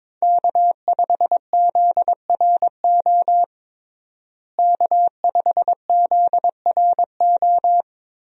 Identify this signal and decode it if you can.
Morse code, heard as a keyed tone, about 22 wpm, 700 Hz. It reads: K5ZRO K5ZRO
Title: USA Call Signs - 5 Characters - Repeated 3x - 22wpm